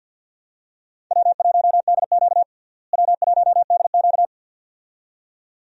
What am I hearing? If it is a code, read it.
Morse code, heard as a keyed tone, about 47 wpm, 700 Hz. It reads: W1DQ W1DQ
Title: USA Call Signs - 4 Characters - Repeated 2x - 50wpm